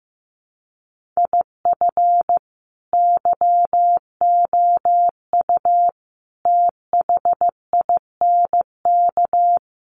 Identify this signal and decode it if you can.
Morse code, heard as a keyed tone, about 14 wpm, 700 Hz. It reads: IF YOU THINK
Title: Sets of 5 Words Encore2 15wpm